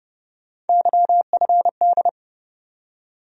Morse code, heard as a keyed tone, about 30 wpm, 700 Hz. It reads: YFD
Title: USA Call Sign Postfixes - 3 Letters 30wpm